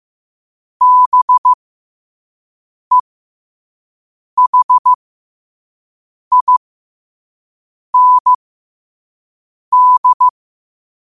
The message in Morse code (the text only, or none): BEHIND